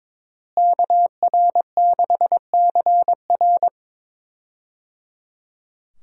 Morse code, heard as a keyed tone, about 22 wpm, 700 Hz. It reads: KR6CR
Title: USA Call Signs - 5 Characters - Repeated 3x - 22wpm